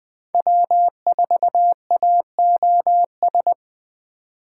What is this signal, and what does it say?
Morse code, heard as a keyed tone, about 21 wpm, 700 Hz. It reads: W4AOS